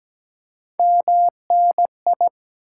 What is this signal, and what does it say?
Morse code, heard as a keyed tone, about 17 wpm, 700 Hz. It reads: MNI
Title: Top 90 QSO Elements 17wpm